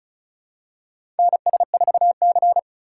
Morse code, heard as a keyed tone, about 36 wpm, 700 Hz. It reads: NS4C
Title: USA Call Signs - 4 Characters - 6x Character Spacing - 35wpm